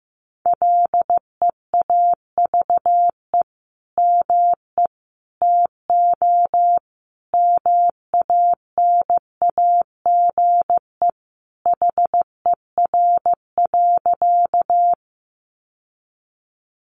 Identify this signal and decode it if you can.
Morse code, heard as a keyed tone, about 15 wpm, 700 Hz. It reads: LEAVE ME TO MANAGE HER.
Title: Sentences from Top 1400 Words 15wpm